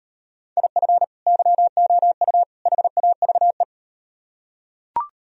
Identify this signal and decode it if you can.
Morse code, heard as a keyed tone, about 38 wpm, 700 Hz. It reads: IF YOU HAVE